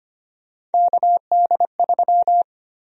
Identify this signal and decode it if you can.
Morse code, heard as a keyed tone, about 25 wpm, 700 Hz. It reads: KD3
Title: USA Call Sign Prefixes - 25wpm